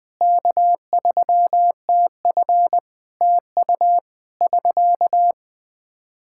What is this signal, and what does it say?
Morse code, heard as a keyed tone, about 20 wpm, 700 Hz. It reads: K3TF TU <SK>
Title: Practice QSO - Exchange 4 - Ending - 20wpm